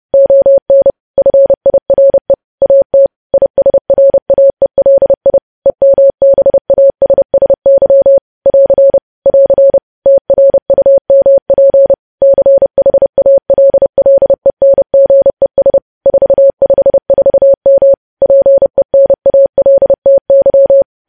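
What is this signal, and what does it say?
Morse code, heard as a keyed tone, about 30 wpm, 550 Hz. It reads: ON FIRE AT ISRAELI EMBASSY <AR> <AR> TRUMP CHALLENGES 454M PENALTY
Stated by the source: News headlines sent in Morse code at 30.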